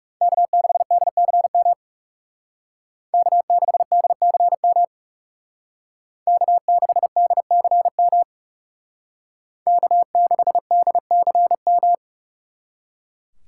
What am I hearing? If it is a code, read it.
Morse code, heard as a keyed tone, about 37 wpm, 700 Hz. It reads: K6DCM K6DCM K6DCM K6DCM
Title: Speed Racing - USA 1x3 Call Signs 30wpm